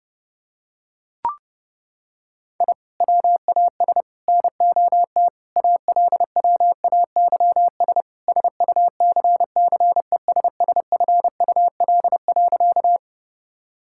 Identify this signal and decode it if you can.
Morse code, heard as a keyed tone, about 30 wpm, 700 Hz. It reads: I WAS NOT ALWAYS SUCCESSFUL.
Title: Sentences from Top 1000 Words - Review Part 6 - 30wpm